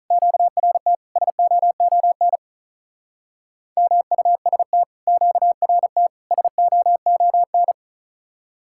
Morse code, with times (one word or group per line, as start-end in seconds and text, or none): 0.1-0.9 QRT
1.2-2.4 SOON
3.8-4.8 MUST
5.1-6.1 QRT
6.3-7.7 SOON